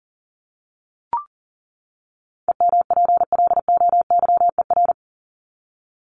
Morse code, heard as a keyed tone, about 40 wpm, 700 Hz. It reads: EMPLOYER
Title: Top 1501-1750 Words 40wpm